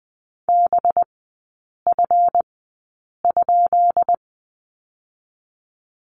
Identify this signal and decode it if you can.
Morse code, heard as a keyed tone, about 20 wpm, spaced as for 8 wpm, 700 Hz. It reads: BF?
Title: 184 Lesson 30e - Letter b - Three Characters 20wpm